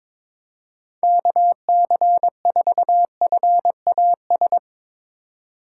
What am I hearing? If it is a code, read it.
Morse code, heard as a keyed tone, about 22 wpm, 700 Hz. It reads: KC4FAS
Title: USA Call Signs - 4x Character Spacing - 22wpm